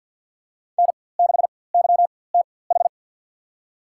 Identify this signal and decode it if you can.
Morse code, heard as a keyed tone, about 50 wpm, spaced as for 23 wpm, 700 Hz. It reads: N6YTH
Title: USA Call Signs - 5 Characters - 4x Character Spacing - 50wpm